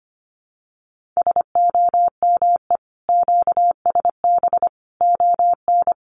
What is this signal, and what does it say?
Morse code, heard as a keyed tone, about 25 wpm, 700 Hz. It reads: SOME QSB ON
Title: Common QSO Phrases 25wpm